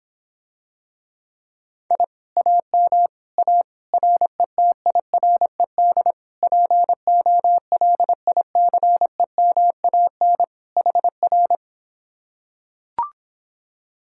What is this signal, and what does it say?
Morse code, heard as a keyed tone, about 26 wpm, 700 Hz. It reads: I AM A RETIRED POLICEMAN HR